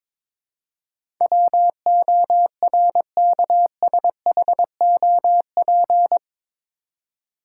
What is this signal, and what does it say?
Morse code, heard as a keyed tone, about 22 wpm, 700 Hz. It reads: WORKSHOP
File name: practice / 22wpm / Eight Letter Words - 22wpm